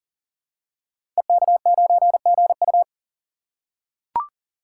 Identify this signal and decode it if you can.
Morse code, heard as a keyed tone, about 40 wpm, 700 Hz. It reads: EK9GU
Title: Code Groups - 5 Letter:Numbers - 40wpm